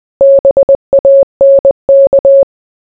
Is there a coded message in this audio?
Yes: Morse code from 0.2 to 2.4 s, heard as a keyed tone, about 20 wpm, 550 Hz. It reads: BANK